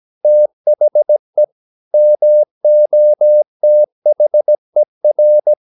Morse code, heard as a keyed tone, about 17 wpm, 600 Hz. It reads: THE MOTHER
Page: Romanen Never af Ken Follet på engelsk.